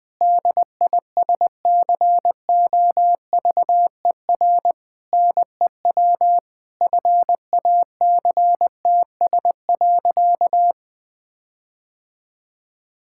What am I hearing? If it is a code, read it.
Morse code, heard as a keyed tone, about 19 wpm, 700 Hz. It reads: DISCOVER NEW FACTS.
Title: Sentences with 5-Word Phrases - 20wpm